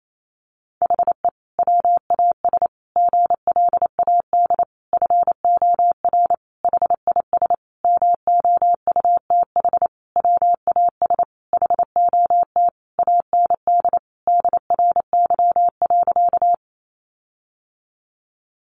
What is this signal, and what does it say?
Morse code, heard as a keyed tone, about 28 wpm, 700 Hz. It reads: HE WAS GLAD FOR HIS MOUTH WAS HOT AND DRY.
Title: Sentences from Top 1500 Words 28wpm